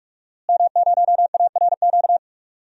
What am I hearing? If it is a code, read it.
Morse code, heard as a keyed tone, about 41 wpm, 700 Hz. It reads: M0ARQ
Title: DXCC-100 Call Signs - 5 Characters - Repeated 2x - 45wpm